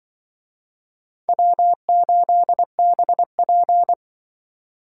Morse code, heard as a keyed tone, about 24 wpm, 700 Hz. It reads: W8BP
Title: Speed Racing - USA 1x2 Call Signs 20wpm